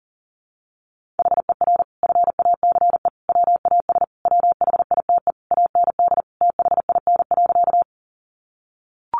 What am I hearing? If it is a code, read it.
Morse code, heard as a keyed tone, about 40 wpm, 700 Hz. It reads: HER FACE WAS WHITE AND THIN.
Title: Sentences from Top 1200 Words 40wpm